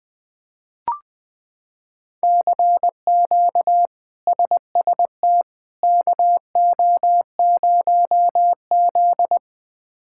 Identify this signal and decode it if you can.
Morse code, heard as a keyed tone, about 20 wpm, 700 Hz. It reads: CQ SST KO0Z
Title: K1USN SST QSO Practice - 20wpm